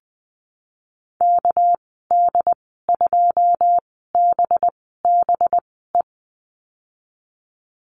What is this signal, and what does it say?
Morse code, heard as a keyed tone, about 20 wpm, spaced as for 15 wpm, 700 Hz. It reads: KD2BBE
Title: 213 Lesson 34g - Number 8 - Call Signs 20wpm